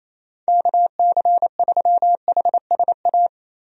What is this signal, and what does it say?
Morse code, heard as a keyed tone, about 27 wpm, 700 Hz. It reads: KC3HSA
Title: USA Call Signs - 6 Characters - 28wpm